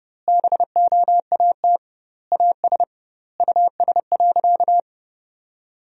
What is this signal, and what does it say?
Morse code, heard as a keyed tone, about 30 wpm, 700 Hz. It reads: BOAT AS US.
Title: 2x Word-Spacing - Sentences from Top 500 Words - 30wpm